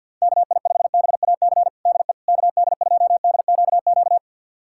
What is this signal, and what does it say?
Morse code, heard as a keyed tone, about 53 wpm, 700 Hz. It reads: KI5BAC DE KD2DQQ
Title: USA Callsign DE Callsign - 6 Characters - 50wpm